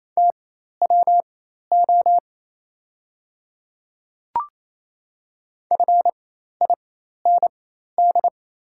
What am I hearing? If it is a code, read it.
Morse code, heard as a keyed tone, about 27 wpm, spaced as for 13 wpm, 700 Hz. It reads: TWO FIND
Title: 4x Character Spacing - Top 200 Words - 28wpm